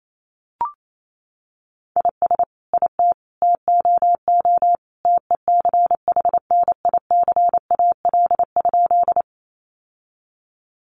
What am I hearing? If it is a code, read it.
Morse code, heard as a keyed tone, about 28 wpm, 700 Hz. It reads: IS IT TOO TECHNICAL?
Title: Sentences from Top 1100 Words 28wpm